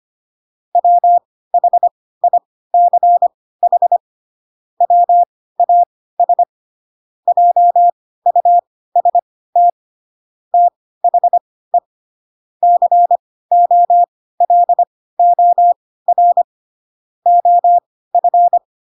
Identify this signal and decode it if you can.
Morse code, heard as a keyed tone, about 25 wpm, spaced as for 16 wpm, 700 Hz. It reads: WHICH WAS JUST THE COLOR OF
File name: Pinocchio / Pinocchio ch2 10 25 ik1hge